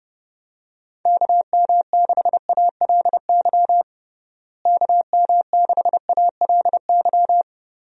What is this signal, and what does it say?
Morse code, heard as a keyed tone, about 30 wpm, 700 Hz. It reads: KM6ALY KM6ALY
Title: USA Call Signs - 6 Characters - Repeated 2x - 30wpm